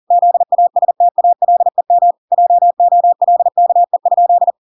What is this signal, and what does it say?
Morse code, heard as a keyed tone, about 40 wpm, 700 Hz. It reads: ZASTALEM JOLKE?